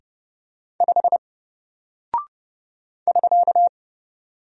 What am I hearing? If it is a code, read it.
Morse code, heard as a keyed tone, about 30 wpm, 700 Hz. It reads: 5 <SK>
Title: Single Character (Letter:Number:Punctuation:Prosign) 30wpm